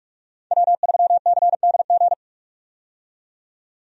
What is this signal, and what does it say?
Morse code, heard as a keyed tone, about 42 wpm, 700 Hz. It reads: W3CDG
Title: USA Call Signs - 5 Characters - 45wpm